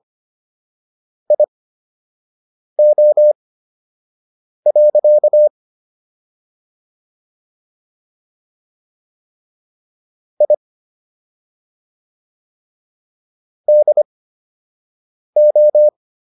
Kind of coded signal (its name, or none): Morse code